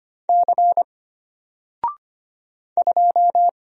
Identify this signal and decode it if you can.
Morse code, heard as a keyed tone, about 25 wpm, 700 Hz. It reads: C 2
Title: Single Letter-Number Alt 25wpm